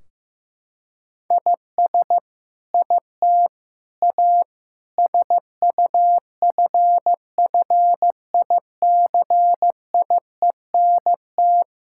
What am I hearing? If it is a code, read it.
Morse code, heard as a keyed tone, about 15 wpm, 700 Hz. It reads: IS IT A SUFFICIENT
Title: Sentences from Top 1600 Words 15wpm